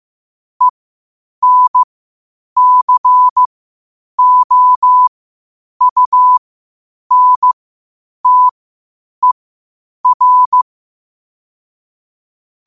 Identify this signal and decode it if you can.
Morse code, heard as a keyed tone, about 15 wpm, spaced as for 8 wpm, 1000 Hz. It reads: ENCOUNTER
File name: qotdfiles / 8wpm